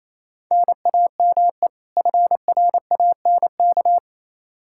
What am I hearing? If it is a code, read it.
Morse code, heard as a keyed tone, about 28 wpm, 700 Hz. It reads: NAME FRANK
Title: Practice QSO - Exchange 1 - Beginning - 28wpm